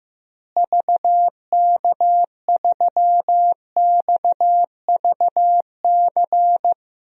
Morse code, heard as a keyed tone, about 16 wpm, 700 Hz. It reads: VK3XVC
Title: DXCC-100 Call Signs - 6 Characters - Repeated 3x - 15wpm